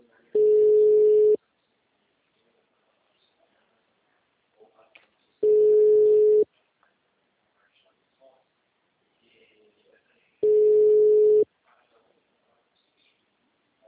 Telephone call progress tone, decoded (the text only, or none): ringback tone